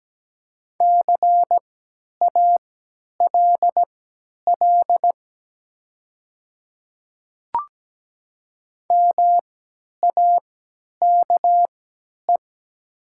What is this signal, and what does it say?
Morse code, heard as a keyed tone, about 17 wpm, spaced as for 10 wpm, 700 Hz. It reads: CALL MAKE
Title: 3x Character Spacing - Top 100 Words - 17wpm